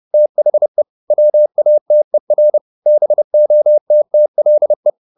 Morse code, heard as a keyed tone, about 30 wpm, 600 Hz. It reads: THE WATER BOTTLE
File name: the old man and the sea / the old man and the sea 1 30wpm